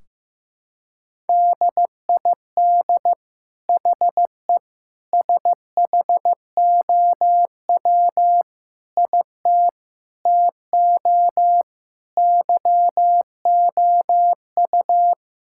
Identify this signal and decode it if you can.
Morse code, heard as a keyed tone, about 15 wpm, 700 Hz. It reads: DID HE SHOW IT TO YOU
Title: Sentences from Top 1600 Words 15wpm